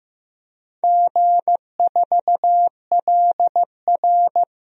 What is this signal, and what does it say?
Morse code, heard as a keyed tone, about 15 wpm, 700 Hz. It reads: G4LR
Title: DXCC-100 Call Signs - 4 Characters - 15wpm